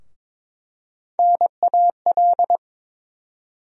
Morse code, heard as a keyed tone, about 22 wpm, 700 Hz. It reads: NAL
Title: USA Call Sign Postfixes - 3 Letters 22wpm